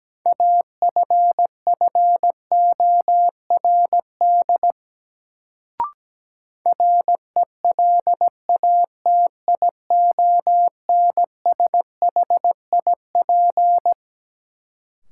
Morse code, heard as a keyed tone, about 17 wpm, 700 Hz. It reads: AFFORD RELATIONSHIP
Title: Sentences from Top 1100 Words - New Word Drill - 17wpm